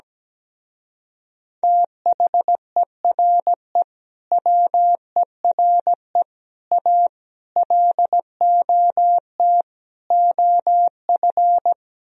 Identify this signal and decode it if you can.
Morse code, heard as a keyed tone, about 17 wpm, 700 Hz. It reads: THERE WERE A LOT OF
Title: Speed-Racing - Sets of 5 Words 17wpm